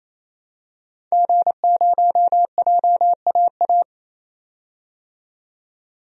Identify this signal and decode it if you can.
Morse code, heard as a keyed tone, about 28 wpm, 700 Hz. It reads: G0JAA
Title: DXCC-100 Call Signs - 5 Characters - 28wpm